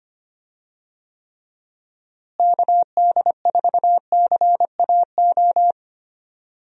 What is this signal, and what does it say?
Morse code, heard as a keyed tone, about 25 wpm, 700 Hz. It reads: KD4CAO
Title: USA Call Signs - 6 Characters - Repeated 2x - 25wpm